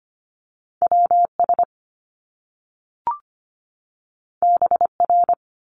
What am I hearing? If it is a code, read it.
Morse code, heard as a keyed tone, about 25 wpm, 700 Hz. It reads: WS BR
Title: Top Two Letter Combinations in Words 25wpm